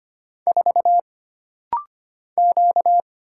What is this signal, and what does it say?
Morse code, heard as a keyed tone, about 25 wpm, 700 Hz. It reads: 4 Q
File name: Single Character (Letter:Number:Punctuation:Prosign) - Rapid-Fire - 25wpm